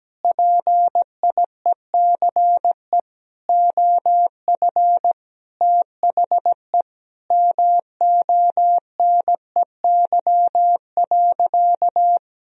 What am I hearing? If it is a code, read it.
Morse code, heard as a keyed tone, about 17 wpm, 700 Hz. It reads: PIECE OF THE MONEY.
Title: Sentences from Top 1000 Words - Review Part 2 - 17wpm